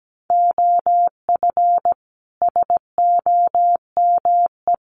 Morse code, heard as a keyed tone, about 17 wpm, 700 Hz. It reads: OF SOME
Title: Sets of 4 Words 17wpm